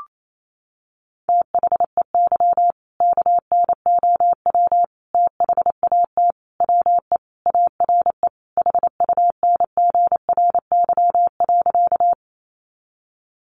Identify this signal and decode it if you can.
Morse code, heard as a keyed tone, about 28 wpm, 700 Hz. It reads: THEY KNOW THAT WE ARE HUNGRY.
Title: Sentences from Top 1400 Words 28wpm